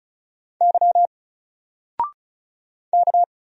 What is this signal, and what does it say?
Morse code, heard as a keyed tone, about 34 wpm, 700 Hz. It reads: Y K
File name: Single-Letter-Number-35wpm